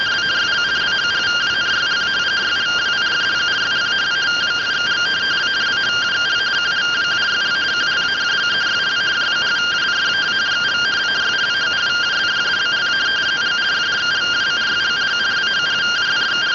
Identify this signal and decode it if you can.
Two-tone teleprinter signal, about 170 Hz shift.
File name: findings / CT 205 ax mips, iDose 3 0007.gen-44 rtty rx